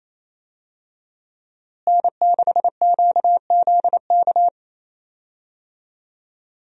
Morse code, heard as a keyed tone, about 27 wpm, 700 Hz. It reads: N6QZK
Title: USA Call Signs - 5 Characters - Repeated 3x - 28wpm